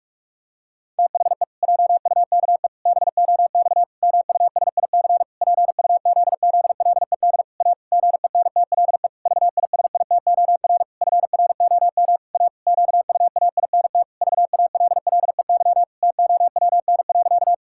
Morse code, heard as a keyed tone, about 46 wpm, 700 Hz. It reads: THE JUKE BOX MUSIC PUZZLED A GENTLE VISITOR FROM A QUAINT VALLEY TOWN.
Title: Pangram Challenge v2 - 45wpm